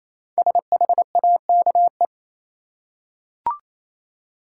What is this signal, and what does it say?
Morse code, heard as a keyed tone, about 28 wpm, 700 Hz. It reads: SHAKE
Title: Five Letter Words - 28wpm